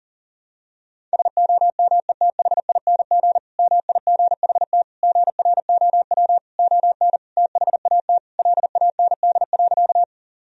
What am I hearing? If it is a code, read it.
Morse code, heard as a keyed tone, about 40 wpm, 700 Hz. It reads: SOMETHING MIGHT GROW ON THAT LAND.
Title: Sentences from Top 1000 Words - Review Part 1 - 40wpm